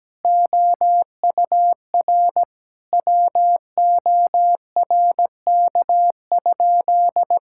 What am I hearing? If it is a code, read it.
Morse code, heard as a keyed tone, about 17 wpm, 700 Hz. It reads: OUR WORK?
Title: Sentences from Top 1000 Words - Review Part 6 - 17wpm